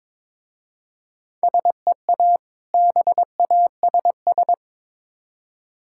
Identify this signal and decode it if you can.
Morse code, heard as a keyed tone, about 22 wpm, 690 Hz. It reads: SEA BASS
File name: Difficult Words with Hints - 22wpm